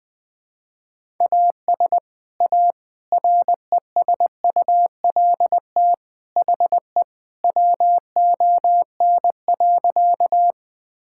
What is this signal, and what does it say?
Morse code, heard as a keyed tone, about 21 wpm, 700 Hz. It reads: AS A RESULT HE WON.
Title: Sentences with 3-Word Phrases - 20wpm